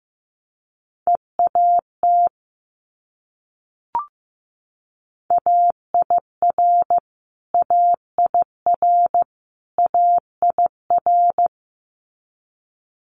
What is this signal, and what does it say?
Morse code, heard as a keyed tone, about 15 wpm, 700 Hz. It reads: EAT AIR AIR AIR
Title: Multi-letter Phonemes Repeated 3x 15wpm